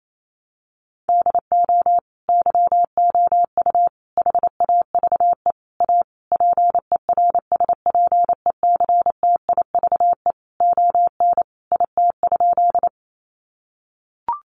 Morse code, heard as a keyed tone, about 28 wpm, 700 Hz. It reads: DO YOU HAVE A PERSPECTIVE ON IT?
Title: Sentences from Top 1200 Words 28wpm